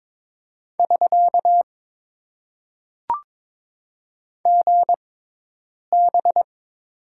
Morse code, heard as a keyed tone, about 22 wpm, spaced as for 8 wpm, 700 Hz. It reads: <SK> GB
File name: Top 90 QSO Elements - 6x Character Spacing - 22wpm